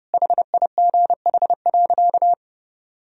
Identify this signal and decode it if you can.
Morse code, heard as a keyed tone, about 29 wpm, 700 Hz. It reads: HIGH.
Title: Sentences from Top 1000 Words - Review Part 2 - 30wpm